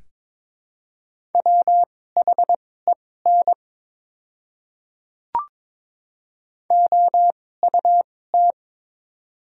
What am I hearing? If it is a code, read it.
Morse code, heard as a keyed tone, about 23 wpm, spaced as for 16 wpm, 700 Hz. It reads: WHEN OUT
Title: 2x Character Spacing - Top 100 Words - 22wpm